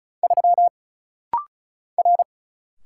Morse code, heard as a keyed tone, about 36 wpm, 700 Hz. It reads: 3 R
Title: Single Letter-Number Alt - Rapid-Fire 35wpm